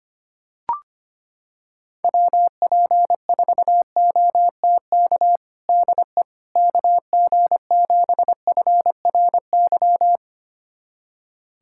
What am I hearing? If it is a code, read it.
Morse code, heard as a keyed tone, about 25 wpm, 700 Hz. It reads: WP4OTK DE KG7FRY
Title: USA Callsign DE Callsign - 6 Characters - 25wpm